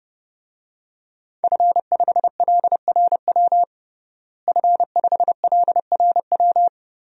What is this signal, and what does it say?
Morse code, heard as a keyed tone, about 30 wpm, 700 Hz. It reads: F5LRW F5LRW
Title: DXCC-100 Call Signs - 5 Characters - Repeated 2x - 30wpm